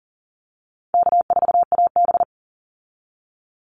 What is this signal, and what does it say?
Morse code, heard as a keyed tone, about 40 wpm, 700 Hz. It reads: K4AB
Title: Call Signs Repeated 2x - 40wpm